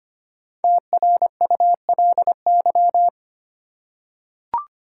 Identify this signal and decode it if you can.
Morse code, heard as a keyed tone, about 26 wpm, 700 Hz. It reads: TRULY
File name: 5 Letter Words - First 3 Letters Given - 25wpm